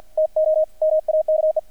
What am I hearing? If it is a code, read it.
Morse code, heard as a keyed tone, about 50 wpm, 630 Hz, with radio noise. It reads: TO MAKE
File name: archived / CRSnet